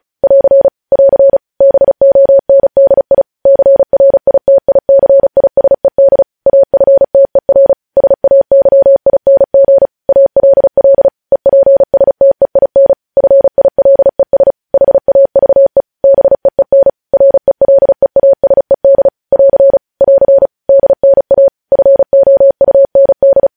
Morse code, heard as a keyed tone, about 36 wpm, 550 Hz. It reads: <AR> <AR> BONDI CRITICISED AFTER SAYING ALL EPSTEIN FILES HAVE BEEN RELEASED <AR> <AR> DNA FOUND